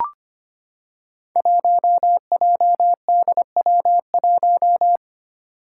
Morse code, heard as a keyed tone, about 25 wpm, 700 Hz. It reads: 1JDW1
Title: Five Challenging Letter-Number Combinations 25wpm